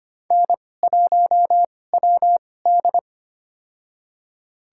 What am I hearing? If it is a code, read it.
Morse code, heard as a keyed tone, about 25 wpm, spaced as for 18 wpm, 700 Hz. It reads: N1WD
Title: USA Call Signs - 4 Characters - 2x Character Spacing - 25wpm